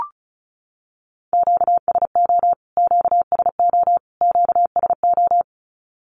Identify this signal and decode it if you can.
Morse code, heard as a keyed tone, about 34 wpm, 700 Hz. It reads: QSO QSO QSO
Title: Q-Codes Repeated 3x 35wpm